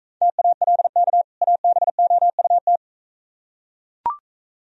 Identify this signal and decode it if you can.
Morse code, heard as a keyed tone, about 42 wpm, 700 Hz. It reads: TALK ABOUT